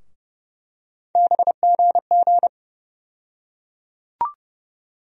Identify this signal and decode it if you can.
Morse code, heard as a keyed tone, about 30 wpm, 700 Hz. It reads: BGG
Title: USA Call Sign Postfixes - 3 Letters 30wpm